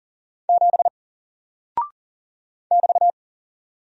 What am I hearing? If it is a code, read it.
Morse code, heard as a keyed tone, about 38 wpm, 700 Hz. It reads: 7 <BT>
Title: Single Character (Letter:Number:Punctuation:Prosign) 40wpm